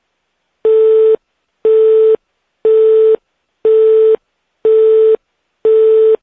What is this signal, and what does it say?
Busy tone (France), 440 Hz.
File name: fpolynesia